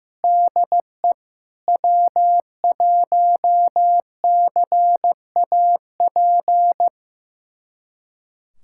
Morse code, heard as a keyed tone, about 15 wpm, 700 Hz. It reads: DE W1CAP
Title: USA Callsign DE Callsign - 5 Characters - 15wpm